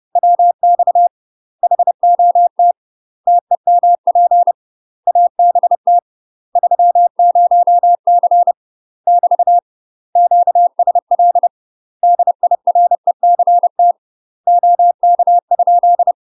Morse code, heard as a keyed tone, about 30 wpm, 700 Hz. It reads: WX HOT TEMP ABT 30C <BT> QSL DIRECT OK?
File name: UK MASTER QSOs TRAINING / 30WPM / UK MASTER QSOs 0041